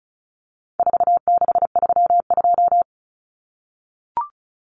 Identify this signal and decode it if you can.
Morse code, heard as a keyed tone, about 34 wpm, 700 Hz. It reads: 4632
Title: Common Cut Numbers - 4-Digit 35wpm